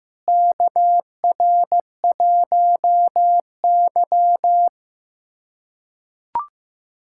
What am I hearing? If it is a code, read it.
Morse code, heard as a keyed tone, about 15 wpm, 700 Hz. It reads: KR1Y
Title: USA Callsign DE Callsign - 4 Characters - 15wpm